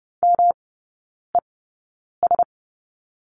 Morse code, heard as a keyed tone, about 30 wpm, spaced as for 9 wpm, 700 Hz. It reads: MES
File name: Lesson 18e - Letter m - Three Characters 30wpm